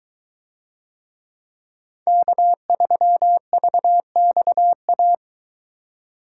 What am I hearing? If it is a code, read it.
Morse code, heard as a keyed tone, about 22 wpm, 700 Hz. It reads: K3VXA